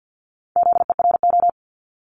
Morse code, heard as a keyed tone, about 50 wpm, 700 Hz. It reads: ZERO
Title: Multi-letter Phonemes 50wpm